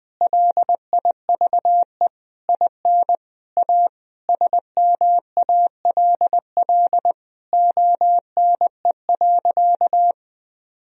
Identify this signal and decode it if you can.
Morse code, heard as a keyed tone, about 20 wpm, 700 Hz. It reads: LIVE IN A SMALL ONE.
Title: Sentences from Top 1000 Words - Review Part 1 - 20wpm